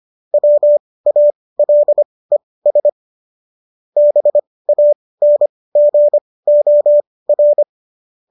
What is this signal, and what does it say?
Morse code, heard as a keyed tone, about 25 wpm, spaced as for 18 wpm, 590 Hz. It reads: WALES BANGOR
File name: qso218 18